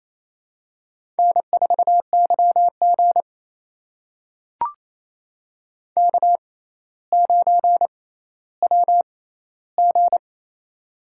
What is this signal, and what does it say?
Morse code, heard as a keyed tone, about 29 wpm, spaced as for 15 wpm, 700 Hz. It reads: N4YG K9WG
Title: USA Call Signs - 4 Characters - 6x Character Spacing - 28wpm